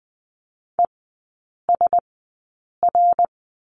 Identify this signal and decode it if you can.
Morse code, heard as a keyed tone, about 20 wpm, spaced as for 8 wpm, 700 Hz. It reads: ESR
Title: 087 Lesson 15e - Number 5 - Three Characters 20wpm